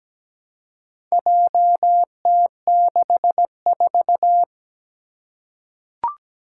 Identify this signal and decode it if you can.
Morse code, heard as a keyed tone, about 17 wpm, 700 Hz. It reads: JT64
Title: Four Challenging Letter-Number Combinations 17wpm